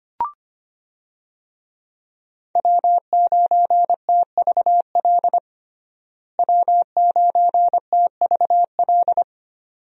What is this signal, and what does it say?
Morse code, heard as a keyed tone, about 25 wpm, 700 Hz. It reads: W9TVL W9TVL
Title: USA Call Signs - 5 Characters - Repeated 2x - 25wpm